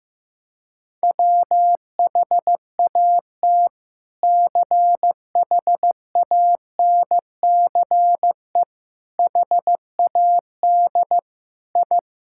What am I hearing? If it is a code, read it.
Morse code, heard as a keyed tone, about 15 wpm, 700 Hz. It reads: WHAT CHANCE HAD I
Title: Sentences from Top 1000 Words - Review Part 6 - 15wpm